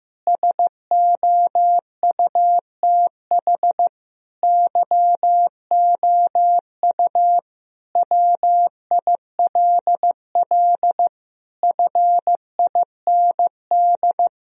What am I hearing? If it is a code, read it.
Morse code, heard as a keyed tone, about 14 wpm, 700 Hz. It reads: SOUTH YOU WILL FIND
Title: Sentences from Top 1000 Words - Review Part 2 - 15wpm